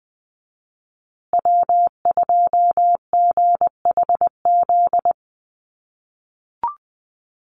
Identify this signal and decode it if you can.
Morse code, heard as a keyed tone, about 20 wpm, 700 Hz. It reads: W2GHZ
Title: USA Call Signs Repeated 2x 20wpm